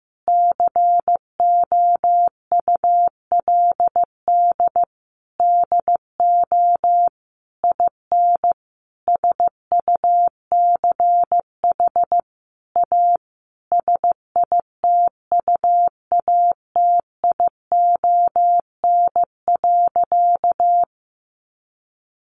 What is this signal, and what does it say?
Morse code, heard as a keyed tone, about 15 wpm, 700 Hz. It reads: COULD DO IN SUCH A SITUATION.
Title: Sentences from Top 1200 Words 15wpm